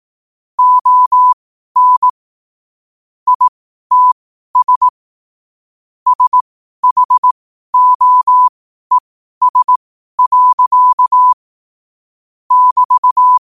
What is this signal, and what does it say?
Morse code, heard as a keyed tone, about 18 wpm, spaced as for 12 wpm, 1000 Hz. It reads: ON ITS SHOES. <BT>
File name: qotdfiles / 12wpm